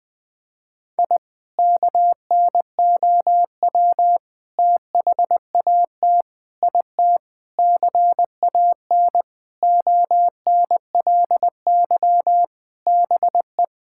Morse code, heard as a keyed tone, about 20 wpm, 700 Hz. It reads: I KNOW THAT IT CAN ONLY BE
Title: Sentences from Top 1000 Words - Review Part 6 - 20wpm